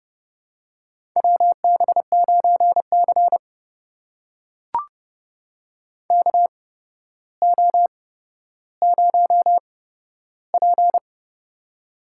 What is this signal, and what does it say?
Morse code, heard as a keyed tone, about 31 wpm, spaced as for 13 wpm, 700 Hz. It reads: WB9C KO0P
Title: USA Call Signs - 4 Characters - 8x Character Spacing - 30wpm